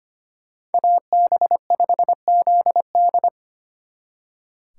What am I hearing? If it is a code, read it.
Morse code, heard as a keyed tone, about 30 wpm, 700 Hz. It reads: AB5ZD
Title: Speed Racing - USA 2x2 Call Signs 25wpm